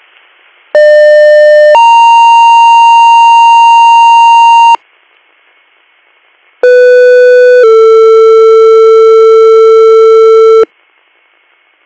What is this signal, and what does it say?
Two-tone page, heard as a one-second tone, then a three-second tone, about 600 then 930 Hz, about 500 then 450 Hz. It is heard from 0.7 to 10.6 s.